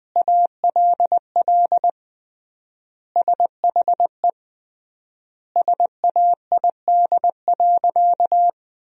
Morse code, heard as a keyed tone, about 20 wpm, 700 Hz. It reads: ALL SHE SAID.
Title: 3x Word-Spacing - Sentences from Top 100 Words - 20wpm